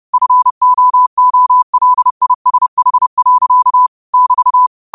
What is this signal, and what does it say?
Morse code, heard as a keyed tone, about 30 wpm, 1000 Hz. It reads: FOOLISH. <BT>
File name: qotdfiles / 30wpm